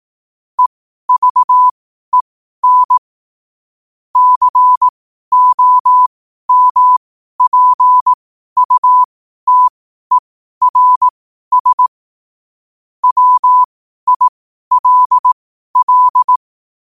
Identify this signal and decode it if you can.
Morse code, heard as a keyed tone, about 18 wpm, spaced as for 12 wpm, 1000 Hz. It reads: EVEN COMPUTERS WILL
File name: qotdfiles / 12wpm